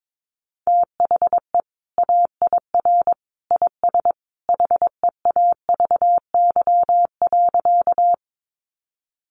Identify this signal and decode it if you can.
Morse code, heard as a keyed tone, about 22 wpm, 700 Hz. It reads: THE AIR IS HEAVY.
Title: Sentences from Top 1100 Words 22wpm